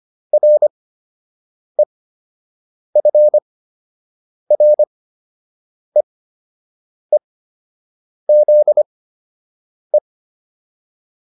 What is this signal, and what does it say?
Morse code, heard as a keyed tone, about 25 wpm, spaced as for 7 wpm, 600 Hz. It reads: REFREEZE
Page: fundamental-practice-files